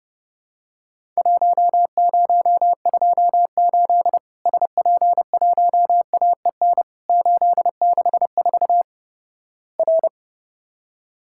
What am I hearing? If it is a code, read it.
Morse code, heard as a keyed tone, about 32 wpm, 700 Hz. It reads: 1028 SP1AEN 864 R
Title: Worked All Europe DX Contest - QTC Exchange - 30wpm